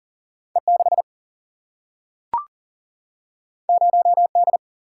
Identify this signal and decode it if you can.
Morse code, heard as a keyed tone, about 40 wpm, 700 Hz. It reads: E 6 0 D
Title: Single-Letter Single-Number 40wpm